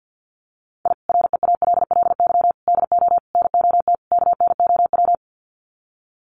Morse code, heard as a keyed tone, about 50 wpm, 700 Hz. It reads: I REALLY DO NOT KNOW
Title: Sets of 5 Words Encore 50wpm